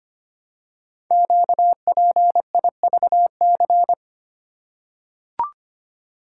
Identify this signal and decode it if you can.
Morse code, heard as a keyed tone, about 25 wpm, 700 Hz. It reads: QPIVC
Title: Code Groups - 5 Letters - 25wpm